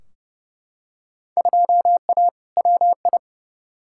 Morse code, heard as a keyed tone, about 32 wpm, 700 Hz. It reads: 2A WI
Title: Field Day Exchanges - 30wpm